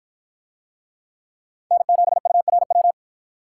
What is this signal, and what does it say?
Morse code, heard as a keyed tone, about 50 wpm, 700 Hz. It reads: N7URW